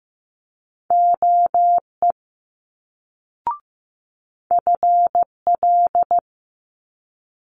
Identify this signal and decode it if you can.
Morse code, heard as a keyed tone, about 15 wpm, 700 Hz. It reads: OE FL
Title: Multi-letter Phonemes Quick Fire 15wpm